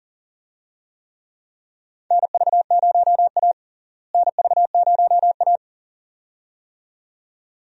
Morse code, heard as a keyed tone, about 40 wpm, 700 Hz. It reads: NV0A NV0A
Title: USA Call Signs - 4 Characters - Repeated 2x - 40wpm